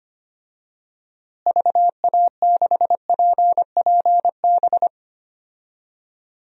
Morse code, heard as a keyed tone, about 25 wpm, 700 Hz. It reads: VA6PPB
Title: DXCC-100 Call Signs - 6 Characters - Repeated 2x - 25wpm